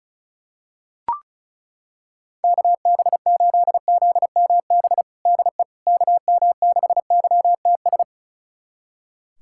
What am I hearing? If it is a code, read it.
Morse code, heard as a keyed tone, about 36 wpm, 700 Hz. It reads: KB8ZMB DE KM6YTS
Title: USA Callsign DE Callsign - 6 Characters - 35wpm